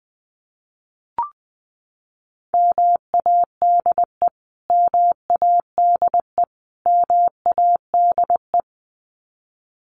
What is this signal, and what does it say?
Morse code, heard as a keyed tone, about 20 wpm, 700 Hz. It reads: MADE MADE MADE
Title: Top 500 Words Repeated 20wpm